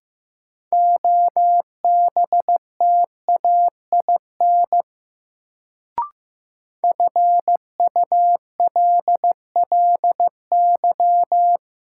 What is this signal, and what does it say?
Morse code, heard as a keyed tone, about 15 wpm, 700 Hz. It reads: OBTAIN FULLY
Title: Sentences from Top 1300 Words - New Word Drill - 15wpm